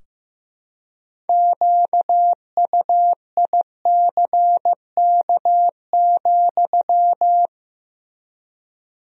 Morse code, heard as a keyed tone, about 16 wpm, 700 Hz. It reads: QUICK,
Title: 3x Word-Spacing - Sentences from Top 500 Words - 15wpm